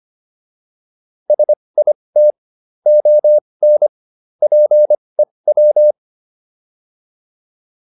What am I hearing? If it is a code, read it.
Morse code, heard as a keyed tone, about 25 wpm, spaced as for 20 wpm, 600 Hz. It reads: SIT ON PEW